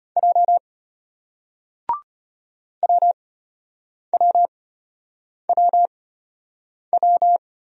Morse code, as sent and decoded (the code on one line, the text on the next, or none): .--- / .-- .-- .-- .--
J WWWW